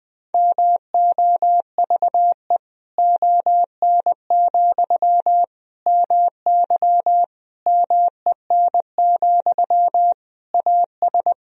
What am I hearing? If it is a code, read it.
Morse code, heard as a keyed tone, about 21 wpm, 700 Hz. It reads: MOVE ON, MY MEN, AS
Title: Sentences from Top 500 Words and Abbreviations 20wpm